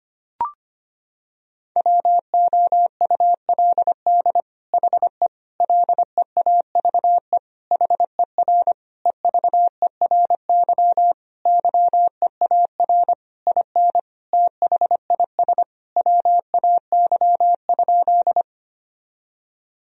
Morse code, heard as a keyed tone, about 25 wpm, 700 Hz. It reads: WOULD HE LEAVE HER EVERY YEAR IN THIS WAY?
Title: Sentences from Top 1000 Words - Review Part 1 - 25wpm